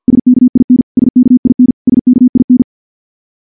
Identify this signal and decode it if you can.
Morse code, heard as a keyed tone, about 49 wpm, 250 Hz. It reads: SKIN SKIN SKIN
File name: morse / morse-training-50WPM